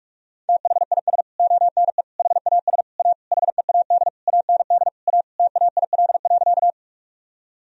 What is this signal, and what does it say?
Morse code, heard as a keyed tone, about 46 wpm, 700 Hz. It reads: THIS ONE HAS A HEAD AND A TAIL.
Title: Sentences from Top 1000 Words - Review Part 3 - 45wpm